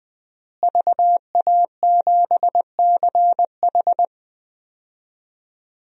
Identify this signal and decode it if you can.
Morse code, heard as a keyed tone, about 20 wpm, 700 Hz. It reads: VA7CH
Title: DXCC-100 Call Signs - 5 Characters - Repeated 2x - 20wpm